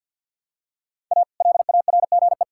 Morse code, heard as a keyed tone, about 47 wpm, 700 Hz. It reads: A LARGE
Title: Pangram Challenge v2 - 50wpm